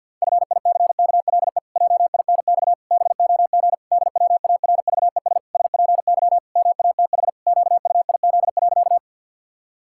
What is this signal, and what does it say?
Morse code, heard as a keyed tone, about 48 wpm, 700 Hz. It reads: FICKLE JINX BOG DWARVES SPY MATH QUIZ.
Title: Pangram Challenge v2 - 50wpm